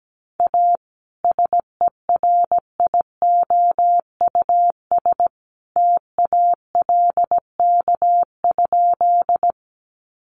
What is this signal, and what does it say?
Morse code, heard as a keyed tone, about 17 wpm, 700 Hz. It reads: A SERIOUS TALK?
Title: Sentences from Top 1100 Words 17wpm